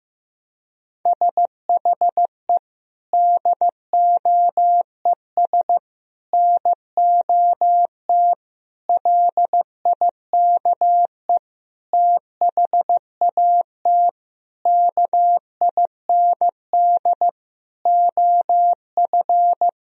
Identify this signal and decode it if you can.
Morse code, heard as a keyed tone, about 16 wpm, 700 Hz. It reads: SHE DOES NOT LIKE THAT KIND OF
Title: Sentences from Top 1000 Words - Review Part 4 - 15wpm